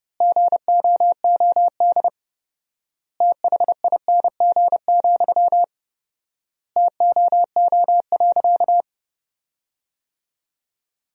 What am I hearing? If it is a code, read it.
Morse code, heard as a keyed tone, about 31 wpm, 700 Hz. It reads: GOOD THING, TOO.
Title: 4x Word-Spacing - Sentences from Top 200 Words - 30wpm